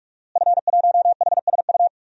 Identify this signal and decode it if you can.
Morse code, heard as a keyed tone, about 46 wpm, 700 Hz. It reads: F1HSU
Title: DXCC-100 Call Signs - 5 Characters - Repeated 2x - 45wpm